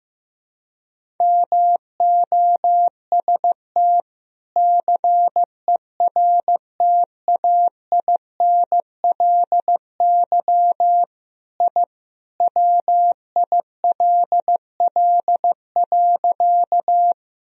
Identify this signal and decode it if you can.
Morse code, heard as a keyed tone, about 15 wpm, 700 Hz. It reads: MOST CERTAINLY I WILL.
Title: Sentences from Top 1000 Words - Review Part 1 - 15wpm